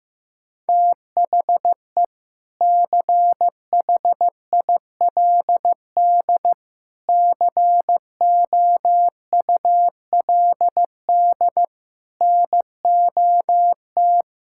Morse code, heard as a keyed tone, about 15 wpm, 700 Hz. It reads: THE CHILD COULD NOT
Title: Sentences from Top 1000 Words - Review Part 6 - 15wpm